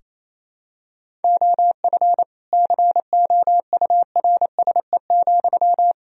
Morse code, heard as a keyed tone, about 27 wpm, 700 Hz. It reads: OF COURSE,
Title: Sentences from Top 1000 Words - Review Part 3 - 28wpm